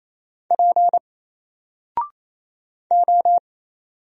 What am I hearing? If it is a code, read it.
Morse code, heard as a keyed tone, about 28 wpm, 700 Hz. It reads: P O
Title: Single Character (Letter:Number:Punctuation:Prosign) Alt 28wpm